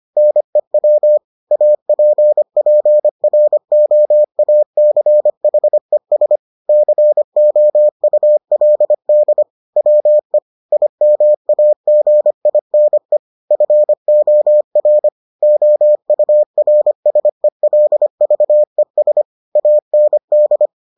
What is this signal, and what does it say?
Morse code, heard as a keyed tone, about 25 wpm, 600 Hz. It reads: NEW APPROACHES COULD WE IMAGINE FOR OURSELVES AND